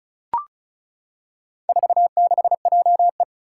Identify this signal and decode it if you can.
Morse code, heard as a keyed tone, about 36 wpm, 700 Hz. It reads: 46JE
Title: Four Challenging Letter-Number Combinations 35wpm